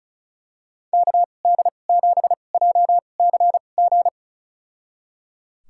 Morse code, heard as a keyed tone, about 34 wpm, spaced as for 25 wpm, 700 Hz. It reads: KD7JCG
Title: USA Call Signs - 2x Character Spacing - 35wpm